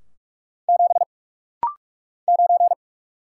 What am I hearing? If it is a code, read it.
Morse code, heard as a keyed tone, about 44 wpm, 700 Hz. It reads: 7 9
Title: Single Letter-Number - Rapid-Fire 45wpm